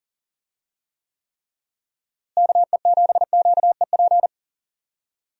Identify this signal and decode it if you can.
Morse code, heard as a keyed tone, about 40 wpm, 700 Hz. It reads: KE7QEP
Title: USA Call Signs - 6 Characters - Repeated 3x - 40wpm